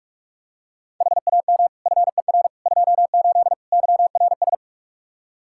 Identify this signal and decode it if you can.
Morse code, heard as a keyed tone, about 42 wpm, 700 Hz. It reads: HAM FER 28 YRS
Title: Practice QSO - Exchange 3 - More Info - 45wpm